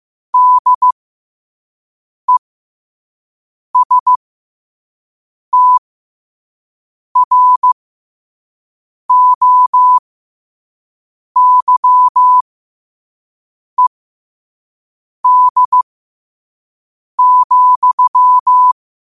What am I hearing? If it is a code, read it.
Morse code, heard as a keyed tone, about 15 wpm, spaced as for 5 wpm, 1000 Hz. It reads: DESTROYED,
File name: qotdfiles / 5wpm